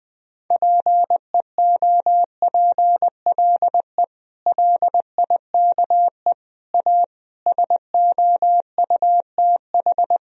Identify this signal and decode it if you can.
Morse code, heard as a keyed tone, about 20 wpm, 700 Hz. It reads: PEOPLE LIKE A SOUTH
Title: Sentences from Top 1000 Words - Review Part 3 - 20wpm